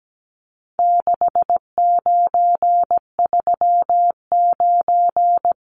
Morse code, heard as a keyed tone, about 17 wpm, 700 Hz. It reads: 6939
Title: Numbers - 4-Digit 17wpm